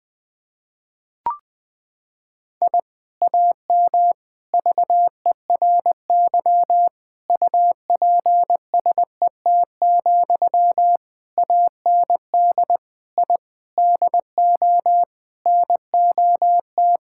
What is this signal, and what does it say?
Morse code, heard as a keyed tone, about 19 wpm, 700 Hz. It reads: I AM VERY UPSET, AND I DO NOT
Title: Sentences from Top 1000 Words - Review Part 5 - 20wpm